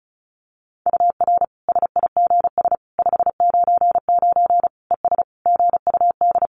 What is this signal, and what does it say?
Morse code, heard as a keyed tone, about 34 wpm, 700 Hz. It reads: UR SIGS 599 ES GUD
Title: Common QSO Phrases 35wpm